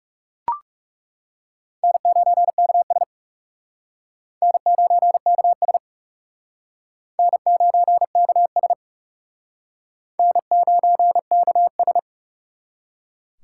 Morse code, heard as a keyed tone, about 40 wpm, 700 Hz. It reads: N9KS N9KS N9KS N9KS